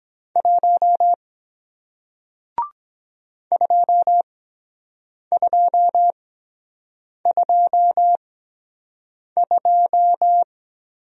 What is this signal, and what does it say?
Morse code, heard as a keyed tone, about 23 wpm, spaced as for 7 wpm, 700 Hz. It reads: 1 2222